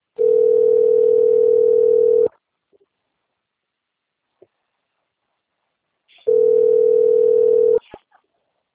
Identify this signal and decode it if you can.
Ringback tone (North America), 440 + 480 Hz.